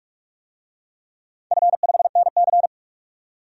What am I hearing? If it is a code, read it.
Morse code, heard as a keyed tone, about 47 wpm, 700 Hz. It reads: F5NC
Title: DXCC-100 Call Signs - 4 Characters - 45wpm